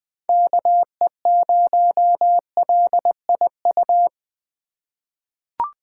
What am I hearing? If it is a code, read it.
Morse code, heard as a keyed tone, about 20 wpm, 700 Hz. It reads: KE0LIU
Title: USA Call Signs - 6 Characters - 2x Character Spacing - 20wpm